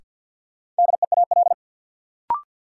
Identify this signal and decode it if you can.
Morse code, heard as a keyed tone, about 51 wpm, 700 Hz. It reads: DEAL
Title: Sentences from Top 800 Words - New Word Drill - 50wpm